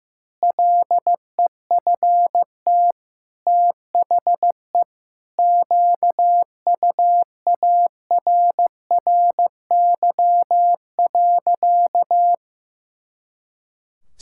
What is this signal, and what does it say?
Morse code, heard as a keyed tone, about 15 wpm, 700 Hz. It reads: LEFT THE QUARRY.
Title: Pangram Challenge v2 - 15wpm